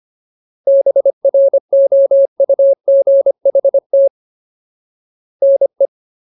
Morse code, heard as a keyed tone, about 25 wpm, 550 Hz. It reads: BROUGHT NE
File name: Ultimate Word List / 25WPM Ultimate